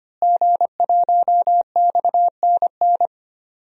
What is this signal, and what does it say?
Morse code, heard as a keyed tone, about 25 wpm, 700 Hz. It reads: G1XNN
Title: DXCC-100 Call Signs - 5 Characters - Repeated 2x - 25wpm